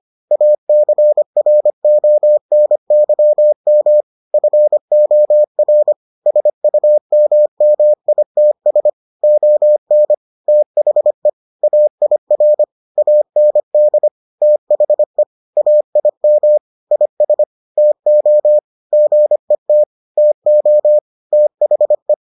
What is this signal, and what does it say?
Morse code, heard as a keyed tone, about 25 wpm, 600 Hz. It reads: ACRONYM FOR SUMMITS ON THE AIR AND THE AIM IS TO GET TO THE